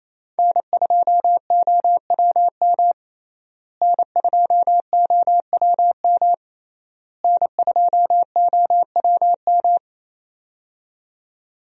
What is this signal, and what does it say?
Morse code, heard as a keyed tone, about 28 wpm, 700 Hz. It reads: N2OWM N2OWM N2OWM
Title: USA Call Signs - 5 Characters - Repeated 3x - 28wpm